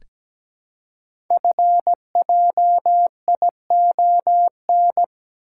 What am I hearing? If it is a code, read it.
Morse code, heard as a keyed tone, about 17 wpm, 700 Hz. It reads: FJION
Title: Code Groups - 5 Letters - 17wpm